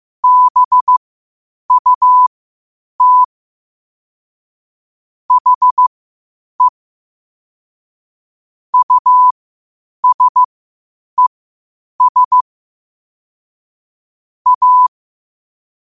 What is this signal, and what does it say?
Morse code, heard as a keyed tone, about 15 wpm, spaced as for 8 wpm, 1000 Hz. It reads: BUT HE USES A
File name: qotdfiles / 8wpm